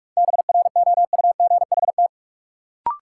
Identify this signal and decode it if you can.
Morse code, heard as a keyed tone, about 47 wpm, 700 Hz. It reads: BROUGHT
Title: Sentences from Top 500 Words - New Word Drill - 45wpm